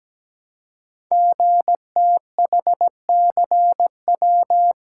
Morse code, heard as a keyed tone, about 17 wpm, 700 Hz. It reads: GTHCW
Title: Code Groups - 5 Letters - 17wpm